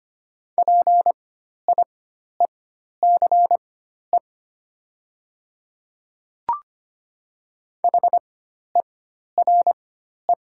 Morse code, heard as a keyed tone, about 25 wpm, spaced as for 12 wpm, 690 Hz. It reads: PIECE HERE
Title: 4x Character Spacing - Top 500 Words - 25wpm